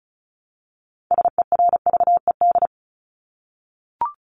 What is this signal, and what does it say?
Morse code, heard as a keyed tone, about 34 wpm, 700 Hz. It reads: SERVED
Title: ED Suffix - Top 100 Words with Base Words 35wpm